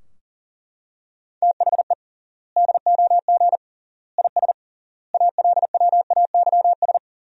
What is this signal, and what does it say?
Morse code, heard as a keyed tone, about 40 wpm, 700 Hz. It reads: THE DOG IS ALWAYS
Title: 3x Word-Spacing - Sentences from Top 400 Words - 40wpm